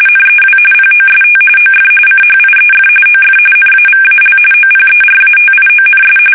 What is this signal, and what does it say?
RTTY, about 45.45 baud. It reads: NEW POST OF CHIEF OF NAVAL OPERATIONS
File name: fsk850